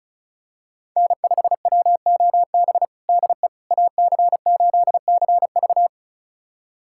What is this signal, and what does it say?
Morse code, heard as a keyed tone, about 36 wpm, 700 Hz. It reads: N5WOB DE AC8CV
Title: USA Callsign DE Callsign - 5 Characters - 35wpm